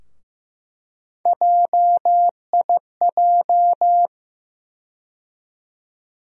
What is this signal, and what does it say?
Morse code, heard as a keyed tone, about 15 wpm, 700 Hz. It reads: JIJ
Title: USA Call Sign Postfixes - 3 Letters 15wpm